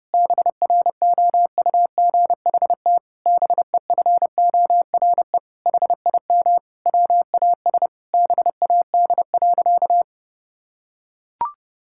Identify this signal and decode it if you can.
Morse code, heard as a keyed tone, about 32 wpm, 700 Hz. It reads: BROUGHT BEFORE HIM WAS BAD.
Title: Sentences from Top 1600 Words 30wpm